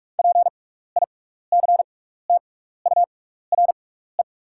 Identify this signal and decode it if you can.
Morse code, heard as a keyed tone, about 44 wpm, spaced as for 15 wpm, 700 Hz. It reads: PICTURE
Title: 6x Character Spacing - Top 200 Words - 45wpm